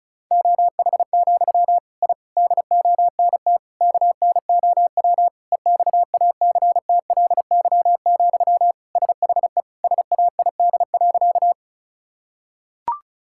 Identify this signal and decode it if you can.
Morse code, heard as a keyed tone, about 36 wpm, 700 Hz. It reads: OH, I DONT KNOW EXACTLY, SHE SAID.
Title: Sentences from Top 1000 Words - Review Part 6 - 35wpm